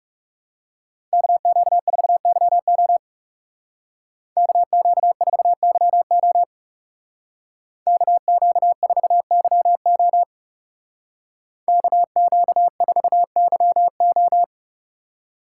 Morse code, heard as a keyed tone, about 37 wpm, 700 Hz. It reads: KQ4YO KQ4YO KQ4YO KQ4YO
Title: Speed Racing - USA 2x2 Call Signs 30wpm